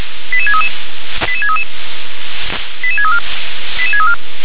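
Five-tone selective call (ZVEI), heard as quick runs of five tones, tones of about 70 ms. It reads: 88633 88633 88634 88634
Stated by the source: Received by SDR, ZVEI five-tone call 88702.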